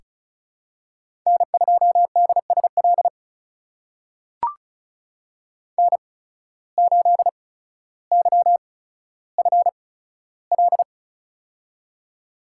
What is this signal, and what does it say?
Morse code, heard as a keyed tone, about 36 wpm, spaced as for 15 wpm, 700 Hz. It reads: N2DSL N8YFL
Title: USA Call Signs - 8x Character Spacing - 35wpm